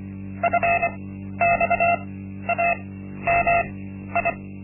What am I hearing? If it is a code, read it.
Morse code, heard as a keyed tone, about 25 wpm, spaced as for 12 wpm, 690 Hz. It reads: FXAMI
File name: Linksmakalnis-1